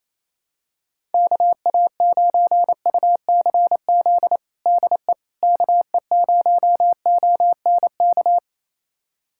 Morse code, heard as a keyed tone, about 28 wpm, 700 Hz. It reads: KA9UCZ DE KE0ONK
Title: USA Callsign DE Callsign - 6 Characters - 28wpm